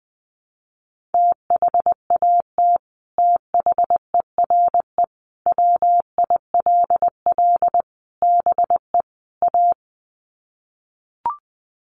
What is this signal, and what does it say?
Morse code, heard as a keyed tone, about 20 wpm, 700 Hz. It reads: THAT THERE WILL BE A
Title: Sets of 5 Words Encore2 20wpm